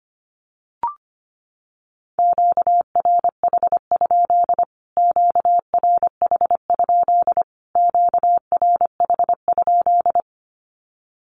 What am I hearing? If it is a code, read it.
Morse code, heard as a keyed tone, about 25 wpm, 700 Hz. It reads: QRH? QRH? QRH?
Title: Q-Codes Repeated 3x 25wpm